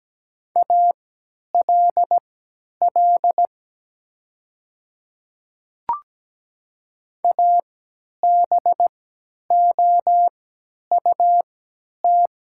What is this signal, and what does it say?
Morse code, heard as a keyed tone, about 17 wpm, spaced as for 10 wpm, 700 Hz. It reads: ALL ABOUT
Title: 3x Character Spacing - Top 100 Words - 17wpm